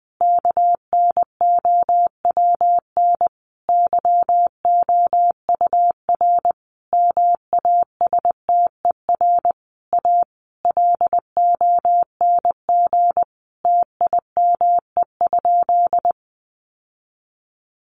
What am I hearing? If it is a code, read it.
Morse code, heard as a keyed tone, about 20 wpm, 700 Hz. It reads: KNOWN YOUR MASTER A LONG TIME?
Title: Sentences from Top 1300 Words 20wpm